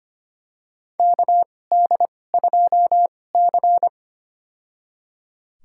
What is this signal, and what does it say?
Morse code, heard as a keyed tone, about 25 wpm, spaced as for 18 wpm, 700 Hz. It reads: KD2C
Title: USA Call Signs - 4 Characters - 2x Character Spacing - 25wpm